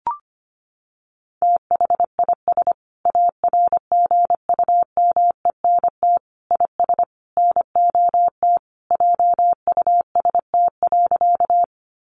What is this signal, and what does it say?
Morse code, heard as a keyed tone, about 25 wpm, 700 Hz. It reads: THIS ARGUMENT IS NOT JUST.
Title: Sentences from Top 1400 Words 25wpm